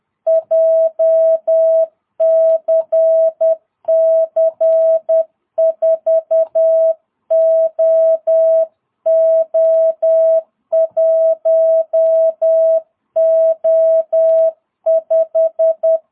Morse code, heard as a keyed tone, about 10 wpm, 660 Hz. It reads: JCC 4OO1O5
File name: jh8cbh / 2022cw